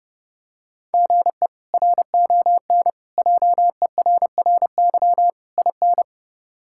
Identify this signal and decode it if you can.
Morse code, heard as a keyed tone, about 30 wpm, 700 Hz. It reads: GE RON JERRY IN
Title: K1USN SST QSO Practice - 30wpm